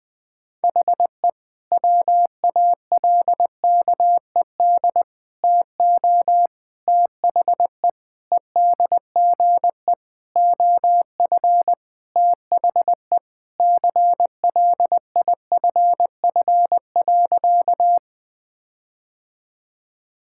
Morse code, heard as a keyed tone, about 21 wpm, 700 Hz. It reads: HE WALKED TO THE EDGE OF THE CLIFF.
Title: Sentences with 5-Word Phrases - 20wpm